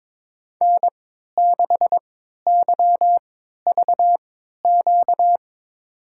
Morse code, heard as a keyed tone, about 22 wpm, spaced as for 13 wpm, 700 Hz. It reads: N6YVQ
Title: USA Call Signs - 5 Characters - 3x Character Spacing - 22wpm